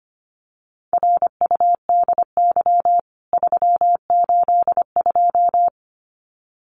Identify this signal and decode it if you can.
Morse code, heard as a keyed tone, about 25 wpm, 700 Hz. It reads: RUDY 382
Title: CWT Exchange 25wpm